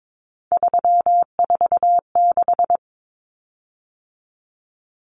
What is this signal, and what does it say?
Morse code, heard as a keyed tone, about 22 wpm, 700 Hz. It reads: 346
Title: Common Cut Numbers - 3-Digit 22wpm